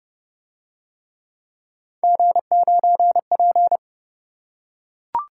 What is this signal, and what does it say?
Morse code, heard as a keyed tone, about 29 wpm, 700 Hz. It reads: G9P